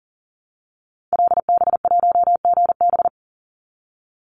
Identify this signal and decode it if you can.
Morse code, heard as a keyed tone, about 40 wpm, 700 Hz. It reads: LB1GB
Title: Call Signs Repeated 3x - 40wpm